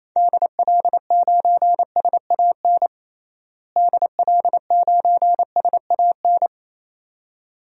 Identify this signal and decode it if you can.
Morse code, heard as a keyed tone, about 28 wpm, 700 Hz. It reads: DL9SAN DL9SAN
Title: DXCC-100 Call Signs - 6 Characters - Repeated 2x - 28wpm